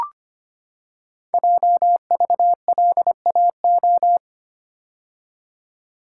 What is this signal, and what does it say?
Morse code, heard as a keyed tone, about 25 wpm, 700 Hz. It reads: JVLAO
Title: Code Groups - 5 Letter:Numbers - 25wpm